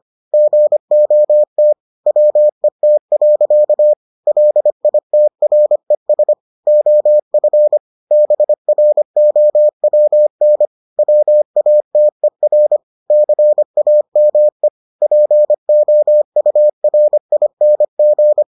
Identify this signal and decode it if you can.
Morse code, heard as a keyed tone, about 25 wpm, 600 Hz. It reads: GOT WET. LITRES OF BROWN WATER CAME POURING